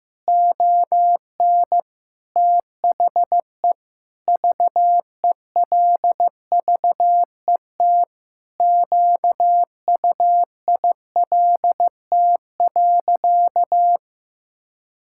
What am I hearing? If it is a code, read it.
Morse code, heard as a keyed tone, about 15 wpm, 700 Hz. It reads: ON THE VELVET QUILT.
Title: Pangram Challenge v2 - 15wpm